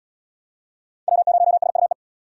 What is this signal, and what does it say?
Morse code, heard as a keyed tone, about 70 wpm, 700 Hz. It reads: W0SUE